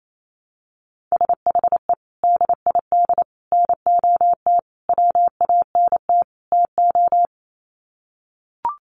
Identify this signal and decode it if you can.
Morse code, heard as a keyed tone, about 28 wpm, 700 Hz. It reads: SHE DID NOT WANT TO
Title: Sets of 5 Words 28wpm